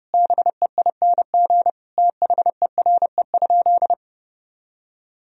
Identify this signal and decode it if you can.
Morse code, heard as a keyed tone, about 32 wpm, 700 Hz. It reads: BEING THERE?
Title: Sentences from Top 1000 Words - Review Part 3 - 30wpm